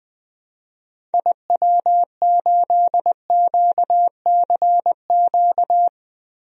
Morse code, heard as a keyed tone, about 21 wpm, 700 Hz. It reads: IW8QCQ
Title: DXCC-100 Call Signs - 6 Characters - Repeated 3x - 20wpm